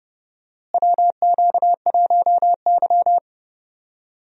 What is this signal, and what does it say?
Morse code, heard as a keyed tone, about 31 wpm, 700 Hz. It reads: WQ1Y
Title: USA Call Signs - 4 Characters - Repeated 2x - 30wpm